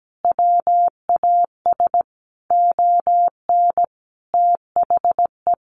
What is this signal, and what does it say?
Morse code, heard as a keyed tone, about 17 wpm, 700 Hz. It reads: WAS ON THE
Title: Sentences from Top 1400 Words 17wpm